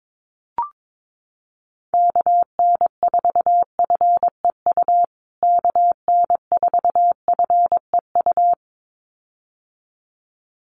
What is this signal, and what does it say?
Morse code, heard as a keyed tone, about 22 wpm, 700 Hz. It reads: KN4FEU KN4FEU
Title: USA Call Signs Repeated 2x 22wpm